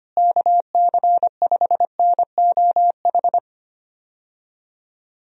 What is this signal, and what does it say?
Morse code, heard as a keyed tone, about 25 wpm, 700 Hz. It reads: KC5NOH
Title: USA Call Signs - 6 Characters - Repeated 3x - 25wpm